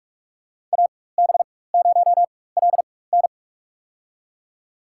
Morse code, heard as a keyed tone, about 46 wpm, spaced as for 21 wpm, 700 Hz. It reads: AB0LN
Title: USA Call Signs - 5 Characters - 4x Character Spacing - 45wpm